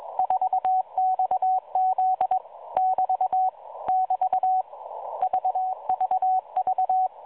Morse code, heard as a keyed tone, about 22 wpm, 750 Hz, with radio noise. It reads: 4XZ <BT> <BT> VVV
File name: 4XZ Shazepe